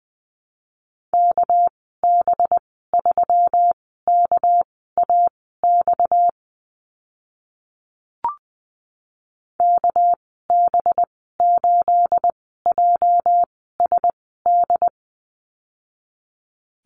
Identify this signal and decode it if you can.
Morse code, heard as a keyed tone, about 20 wpm, spaced as for 14 wpm, 700 Hz. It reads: KB3KAX KB8JSD
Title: 245 Lesson 39g - Letter z - Call Signs 20wpm